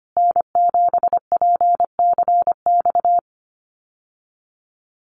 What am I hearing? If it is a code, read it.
Morse code, heard as a keyed tone, about 25 wpm, 700 Hz. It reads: N7PCX
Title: USA Call Signs Repeated 3x 25wpm